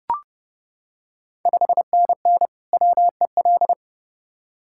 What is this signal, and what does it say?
Morse code, heard as a keyed tone, about 30 wpm, 700 Hz. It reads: 5NN WEL
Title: Ontario QSO Party Exchange 30wpm